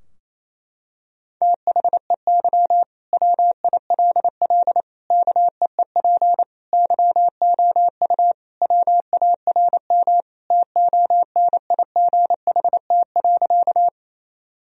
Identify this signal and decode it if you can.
Morse code, heard as a keyed tone, about 28 wpm, 700 Hz. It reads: THEY WILL KEEP YOU WARM TONIGHT.
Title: Sentences from Top 1000 Words - Review Part 3 - 28wpm